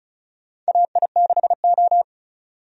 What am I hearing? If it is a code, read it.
Morse code, heard as a keyed tone, about 36 wpm, 700 Hz. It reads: AI6O
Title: USA Call Signs - 4 Characters - 35wpm